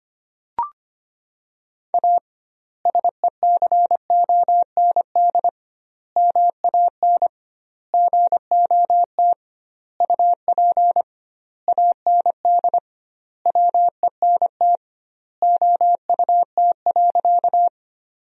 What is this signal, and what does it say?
Morse code, heard as a keyed tone, about 25 wpm, 700 Hz. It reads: A SECOND MAN GOT UP AND WENT OUT.
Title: 2x Word-Spacing - Sentences from Top 300 Words - 25wpm